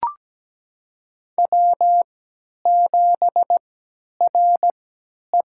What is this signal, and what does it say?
Morse code, heard as a keyed tone, about 17 wpm, spaced as for 10 wpm, 700 Hz. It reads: W7RE
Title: USA Call Signs - 4 Characters - 3x Character Spacing - 17wpm